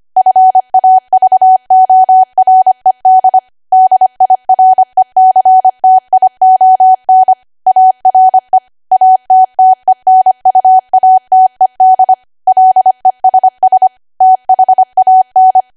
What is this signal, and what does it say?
Morse code, heard as a keyed tone, about 25 wpm, 750 Hz. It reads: FAVORED DIRECTION ARE ATTENUATED LESS THAN
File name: Archive / 25 WPM / 25WPM